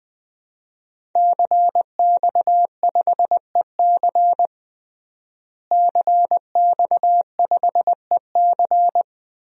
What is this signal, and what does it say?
Morse code, heard as a keyed tone, about 21 wpm, 700 Hz. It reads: CX5EC CX5EC
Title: DXCC-100 Call Signs - 5 Characters - Repeated 3x - 20wpm